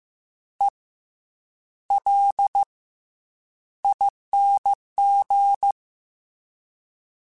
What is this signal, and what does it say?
Morse code, heard as a keyed tone, about 15 wpm, 800 Hz. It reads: E L ING
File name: Special INGs-mp3s / Special INGs-mp3s / Fast ing @13